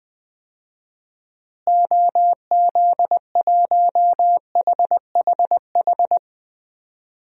Morse code, heard as a keyed tone, about 20 wpm, 700 Hz. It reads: OZ1HHH
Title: DXCC-100 Call Signs - 6 Characters - Repeated 2x - 20wpm